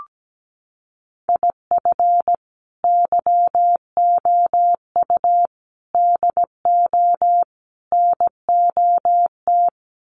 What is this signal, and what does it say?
Morse code, heard as a keyed tone, about 17 wpm, 700 Hz. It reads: IF YOU DO NOT
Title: Sets of 4 Words 17wpm